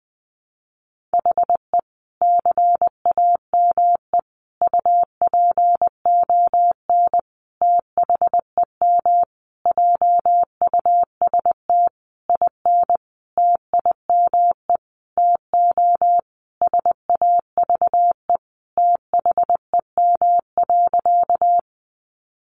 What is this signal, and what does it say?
Morse code, heard as a keyed tone, about 20 wpm, 700 Hz. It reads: HE CAME UPON THEM JUST IN TIME TO SAVE THEM.
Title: Sentences from Top 1100 Words 20wpm